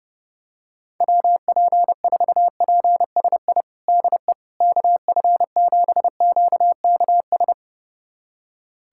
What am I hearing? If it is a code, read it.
Morse code, heard as a keyed tone, about 30 wpm, 700 Hz. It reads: WP4PSI DE KF7QKS
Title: USA Callsign DE Callsign - 6 Characters - 30wpm